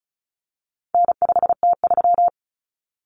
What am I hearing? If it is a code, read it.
Morse code, heard as a keyed tone, about 34 wpm, 700 Hz. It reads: N5T3
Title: Common Cut Numbers - 4-Digit 35wpm